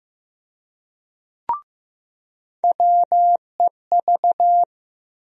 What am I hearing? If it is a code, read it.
Morse code, heard as a keyed tone, about 15 wpm, 700 Hz. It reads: WEV
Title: USA Call Sign Postfixes - 3 Letters 15wpm